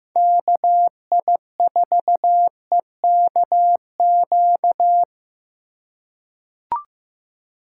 Morse code, heard as a keyed tone, about 15 wpm, 700 Hz. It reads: KI4EKQ
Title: USA Call Signs - 6 Characters - 3x Character Spacing - 15wpm